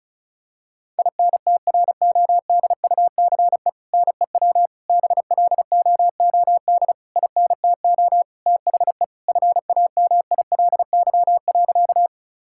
Morse code, heard as a keyed tone, about 34 wpm, 700 Hz. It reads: INTRODUCE NEW BLOOD INTO THE FAMILY.
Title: Sentences from Top 1600 Words 35wpm